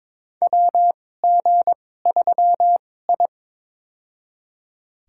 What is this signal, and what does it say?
Morse code, heard as a keyed tone, about 22 wpm, spaced as for 16 wpm, 700 Hz. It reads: WG3I
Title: USA Call Signs - 4 Characters - 2x Character Spacing - 22wpm